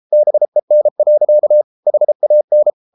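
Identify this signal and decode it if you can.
Morse code, heard as a keyed tone, about 33 wpm, 600 Hz. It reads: BEN. HAN
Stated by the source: Morse code at 33 wpm, QRQ.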